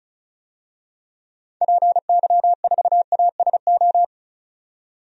Morse code, heard as a keyed tone, about 36 wpm, 700 Hz. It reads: PY4ASO
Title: DXCC-100 Call Signs - 6 Characters - 35wpm